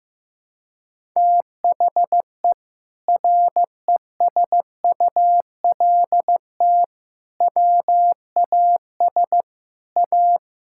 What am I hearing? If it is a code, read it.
Morse code, heard as a keyed tone, about 15 wpm, 700 Hz. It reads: THE RESULT WAS A
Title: Sentences from Top 1000 Words - Review Part 6 - 15wpm